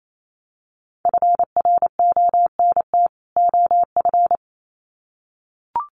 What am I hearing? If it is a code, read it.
Morse code, heard as a keyed tone, about 28 wpm, 700 Hz. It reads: FRONT OF
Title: Sets of 2 Words 28wpm